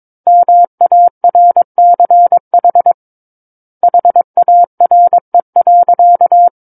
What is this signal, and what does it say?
Morse code, heard as a keyed tone, about 22 wpm, 700 Hz. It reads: MARCH HARE.
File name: 06-Chapter-06 22 WPM